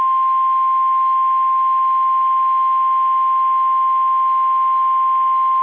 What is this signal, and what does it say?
Emergency Alert System (SAME) tones (the 1050 Hz attention tone). It is heard from 0.4 to 5.1 s.
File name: eas / KSDZKDJL / KSDZKDJL